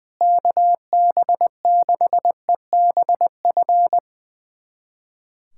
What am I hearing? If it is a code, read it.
Morse code, heard as a keyed tone, about 21 wpm, 700 Hz. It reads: KB6EBF
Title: USA Callsign DE Callsign 20wpm